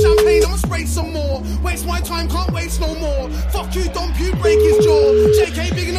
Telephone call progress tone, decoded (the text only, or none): ringback tone